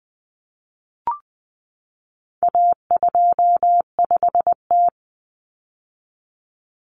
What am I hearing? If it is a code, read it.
Morse code, heard as a keyed tone, about 20 wpm, 700 Hz. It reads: A25T
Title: Common Cut Numbers - 4-Digit 20wpm